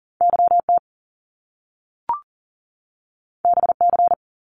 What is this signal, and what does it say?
Morse code, heard as a keyed tone, about 40 wpm, 700 Hz. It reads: YT BC
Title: Canadian Provinces and Territories 40wpm